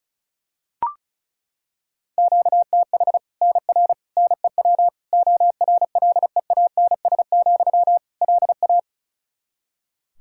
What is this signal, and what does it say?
Morse code, heard as a keyed tone, about 36 wpm, 700 Hz. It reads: QTH NR NEW ORLEANS, LA
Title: K1USN SST QSO Practice 35wpm